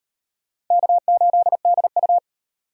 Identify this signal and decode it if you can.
Morse code, heard as a keyed tone, about 34 wpm, 700 Hz. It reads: K8DU